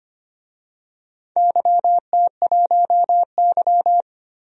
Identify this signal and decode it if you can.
Morse code, heard as a keyed tone, about 25 wpm, 700 Hz. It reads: YT1Y
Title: DXCC-100 Call Signs - 4 Characters - Repeated 2x - 25wpm